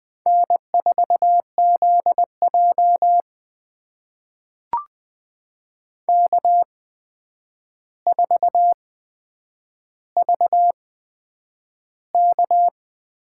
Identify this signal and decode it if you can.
Morse code, heard as a keyed tone, about 20 wpm, spaced as for 9 wpm, 700 Hz. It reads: N4ZJ K4VK
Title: USA Call Signs - 4 Characters - 8x Character Spacing - 20wpm